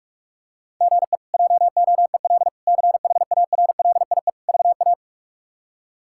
Morse code, heard as a keyed tone, about 46 wpm, 700 Hz. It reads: GE JOEL CHARLIE VA
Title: K1USN SST QSO Practice - 45wpm